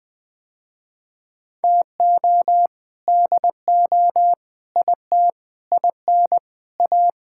Morse code, heard as a keyed tone, about 20 wpm, 700 Hz. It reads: TO DO IT IN A
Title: Sentences with 7-Word Phrases - Spoken Beginning - 20wpm